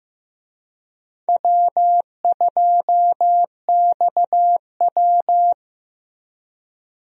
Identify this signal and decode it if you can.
Morse code, heard as a keyed tone, about 14 wpm, 700 Hz. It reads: W2XW
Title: USA Call Signs - 4 Characters - Repeated 3x - 15wpm